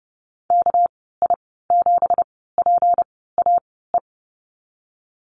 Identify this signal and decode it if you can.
Morse code, heard as a keyed tone, about 30 wpm, spaced as for 17 wpm, 700 Hz. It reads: KI7PAE
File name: Lesson 37g - Letter x - Call Signs 30wpm